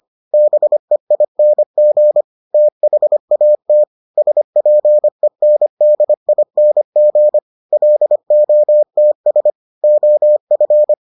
Morse code, heard as a keyed tone, about 25 wpm, 600 Hz. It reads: BEING THAT SPENDING LOTS OF